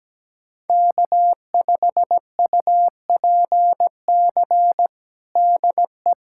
Morse code, heard as a keyed tone, about 17 wpm, 700 Hz. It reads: K5UPC DE
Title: USA Callsign DE Callsign 17wpm